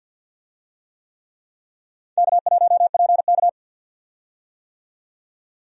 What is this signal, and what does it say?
Morse code, heard as a keyed tone, about 53 wpm, 700 Hz. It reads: K1PK
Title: USA Call Signs - 4 Characters - 50wpm